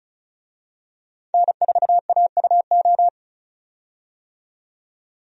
Morse code, heard as a keyed tone, about 36 wpm, 700 Hz. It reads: N4AUO
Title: USA Call Signs - 5 Characters - 35wpm